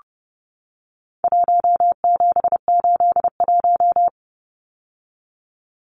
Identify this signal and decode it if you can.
Morse code, heard as a keyed tone, about 30 wpm, 700 Hz. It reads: 1781
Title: Numbers - Contest Style - 4-Digit 30wpm